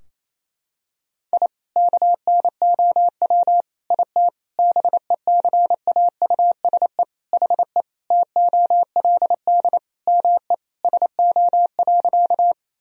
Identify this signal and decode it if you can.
Morse code, heard as a keyed tone, about 27 wpm, 700 Hz. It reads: I KNOW IT BECAUSE HE TOLD ME SO.
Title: Sentences from Top 1000 Words - Review Part 3 - 28wpm